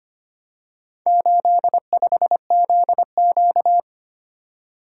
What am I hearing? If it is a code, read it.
Morse code, heard as a keyed tone, about 25 wpm, 700 Hz. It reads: 85ZQ
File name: practice / 25wpm / Four Letter-Number Combinations - 25wpm